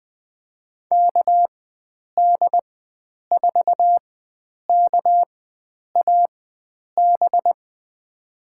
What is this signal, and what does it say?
Morse code, heard as a keyed tone, about 20 wpm, spaced as for 9 wpm, 700 Hz. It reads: KD4KAB
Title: USA Call Signs - 6 Characters - 4x Character Spacing - 20wpm